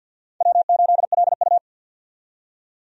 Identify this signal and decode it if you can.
Morse code, heard as a keyed tone, about 53 wpm, 700 Hz. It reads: W8LU
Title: USA Call Signs - 4 Characters - 50wpm